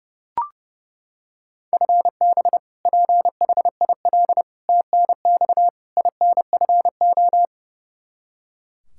Morse code, heard as a keyed tone, about 29 wpm, 700 Hz. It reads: FB PHIL TNX INFO
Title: Practice QSO - Exchange 3 - More Info - 30wpm